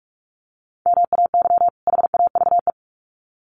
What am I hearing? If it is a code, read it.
Morse code, heard as a keyed tone, about 44 wpm, 700 Hz. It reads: MAY HAVE
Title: Sets of 2 Words 45wpm